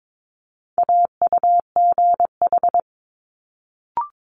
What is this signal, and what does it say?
Morse code, heard as a keyed tone, about 22 wpm, 700 Hz. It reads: AUGH
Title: Multi-letter Phonemes Quick Fire 22wpm